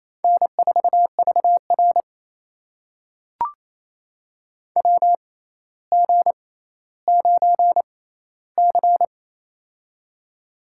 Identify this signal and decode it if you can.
Morse code, heard as a keyed tone, about 28 wpm, spaced as for 14 wpm, 700 Hz. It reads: N4VR WG9C
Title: USA Call Signs - 4 Characters - 6x Character Spacing - 28wpm